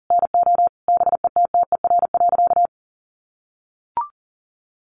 Morse code, heard as a keyed tone, about 40 wpm, 700 Hz. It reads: NO BETTER.
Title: Sentences from Top 1300 Words 40wpm